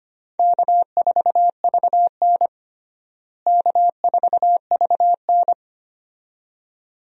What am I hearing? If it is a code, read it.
Morse code, heard as a keyed tone, about 25 wpm, 700 Hz. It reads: K4VN K4VN
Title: USA Call Signs - 4 Characters - Repeated 2x - 25wpm